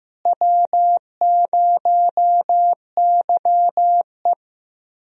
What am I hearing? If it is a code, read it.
Morse code, heard as a keyed tone, about 15 wpm, 700 Hz. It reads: W0YE
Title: USA Call Signs - 4 Characters - Repeated 3x - 15wpm